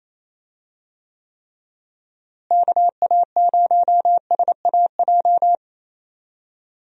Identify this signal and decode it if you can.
Morse code, heard as a keyed tone, about 28 wpm, 700 Hz. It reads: KA0SAJ
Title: USA Call Signs - 6 Characters - Repeated 3x - 28wpm